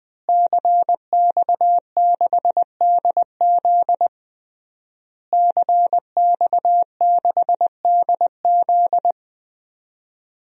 Morse code, heard as a keyed tone, about 20 wpm, 700 Hz. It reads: CX6DZ CX6DZ
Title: DXCC-100 Call Signs - 5 Characters - Repeated 3x - 20wpm